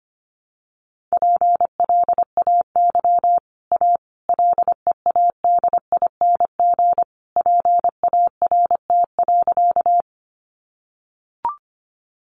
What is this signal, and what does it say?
Morse code, heard as a keyed tone, about 25 wpm, 700 Hz. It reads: PLAY A LEADING PART.
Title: Sentences from Top 1100 Words 25wpm